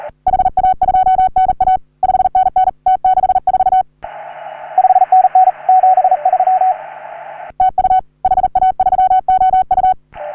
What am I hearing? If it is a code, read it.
Morse code, heard as a keyed tone, about 39 wpm, 750 Hz, with radio noise. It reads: HA2NA 5NN T64 5NN 73 TU HA3OU
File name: CW / HA3OU RHOBCW REC